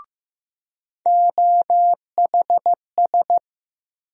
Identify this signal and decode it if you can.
Morse code, heard as a keyed tone, about 15 wpm, 700 Hz. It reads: OHS
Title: USA Call Sign Postfixes - 3 Letters 15wpm